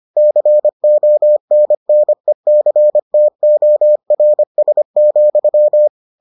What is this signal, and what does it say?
Morse code, heard as a keyed tone, about 25 wpm, 600 Hz. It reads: CONNECTORS,